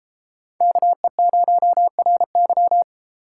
Morse code, heard as a keyed tone, about 31 wpm, 700 Hz. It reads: KE0RY